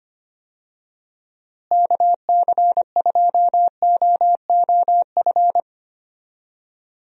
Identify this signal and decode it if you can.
Morse code, heard as a keyed tone, about 25 wpm, 700 Hz. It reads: KC2OOF
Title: USA Call Signs - 6 Characters - Repeated 3x - 25wpm